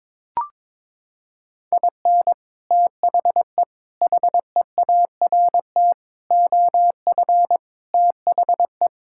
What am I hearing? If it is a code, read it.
Morse code, heard as a keyed tone, about 22 wpm, 700 Hz. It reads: IN THE HEART OF THE
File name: practice / 22wpm / Sentences with 3-Word Phrases - 22wpm